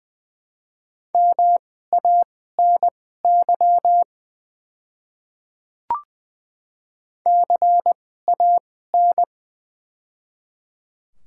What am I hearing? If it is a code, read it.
Morse code, heard as a keyed tone, about 20 wpm, spaced as for 14 wpm, 700 Hz. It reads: MANY CAN
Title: 2x Character Spacing - Top 200 Words - 20wpm